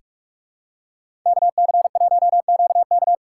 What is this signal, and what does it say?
Morse code, heard as a keyed tone, about 46 wpm, 700 Hz. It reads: KC1QK
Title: USA Callsign DE Callsign - 5 Characters - 45wpm